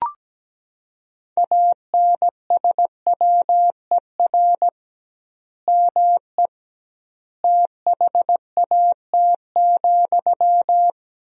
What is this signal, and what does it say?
Morse code, heard as a keyed tone, about 17 wpm, 700 Hz. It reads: ANSWER ME THAT,
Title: 2x Word-Spacing - Sentences from Top 500 Words - 17wpm